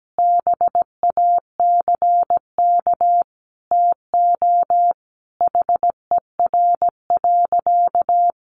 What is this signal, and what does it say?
Morse code, heard as a keyed tone, about 17 wpm, 700 Hz. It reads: BACK TO HER.
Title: Sentences from Top 300 Words 17wpm